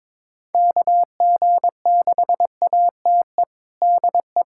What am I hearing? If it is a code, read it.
Morse code, heard as a keyed tone, about 22 wpm, 700 Hz. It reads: KG6ATE DE
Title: USA Callsign DE Callsign 22wpm